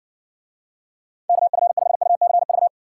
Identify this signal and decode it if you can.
Morse code, heard as a keyed tone, about 60 wpm, 700 Hz. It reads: KF5UCV